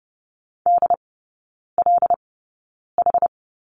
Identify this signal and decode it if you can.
Morse code, heard as a keyed tone, about 30 wpm, spaced as for 9 wpm, 690 Hz. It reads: DLH
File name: Lesson 17e - Letter u - Three Characters 30wpm